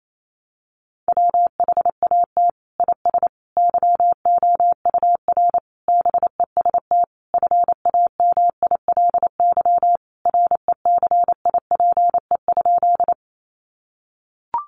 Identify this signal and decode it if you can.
Morse code, heard as a keyed tone, about 28 wpm, 700 Hz. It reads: WHAT IS YOUR BEST FAMILY RECIPE?
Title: Sentences from Top 1200 Words 28wpm